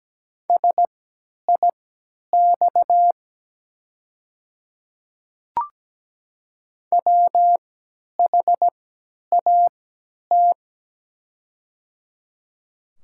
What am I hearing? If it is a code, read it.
Morse code, heard as a keyed tone, about 17 wpm, spaced as for 10 wpm, 700 Hz. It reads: SIX WHAT
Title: 3x Character Spacing - Top 500 Words - 17wpm